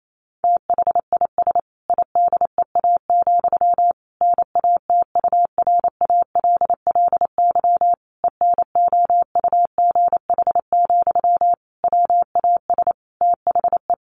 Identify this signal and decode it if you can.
Morse code, heard as a keyed tone, about 28 wpm, 700 Hz. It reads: THIS IDEA, NATURALLY ENOUGH, WAS THE
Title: Sentences from Top 1300 Words 28wpm